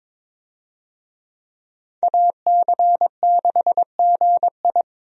Morse code, heard as a keyed tone, about 21 wpm, 700 Hz. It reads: AC6GI
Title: USA Call Signs - 5 Characters - 22wpm